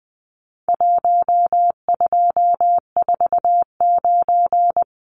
Morse code, heard as a keyed tone, about 20 wpm, 700 Hz. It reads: 1249
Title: Numbers - 4-Digit 20wpm